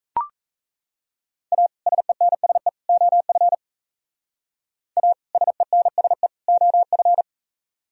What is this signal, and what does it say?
Morse code, heard as a keyed tone, about 35 wpm, 700 Hz. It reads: A SENSE OF A SENSE OF
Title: Speed-Racing - Sets of 3 Words 28wpm